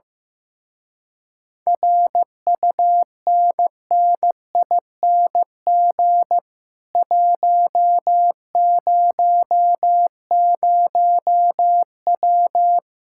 Morse code, heard as a keyed tone, about 15 wpm, 700 Hz. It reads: RUNNING 100W
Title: Speed-Racing Common QSO Phrases Encore - 15wpm